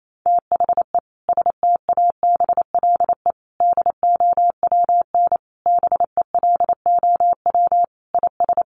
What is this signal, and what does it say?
Morse code, heard as a keyed tone, about 28 wpm, 700 Hz. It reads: THE STABLE DOWN BELOW IS
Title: Sentences from Top 1500 Words 28wpm